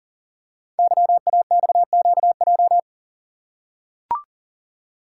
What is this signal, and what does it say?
Morse code, heard as a keyed tone, about 39 wpm, 700 Hz. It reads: YAXQJ
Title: Code Groups - 5 Letters - 40wpm